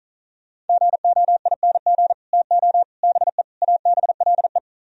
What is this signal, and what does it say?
Morse code, heard as a keyed tone, about 46 wpm, 700 Hz. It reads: GOING TO BE ABLE